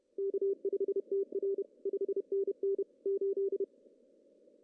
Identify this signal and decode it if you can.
Morse code, heard as a keyed tone, about 30 wpm, 410 Hz. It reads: K5TR 5NN 8
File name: CW / cqww-cw-k5tr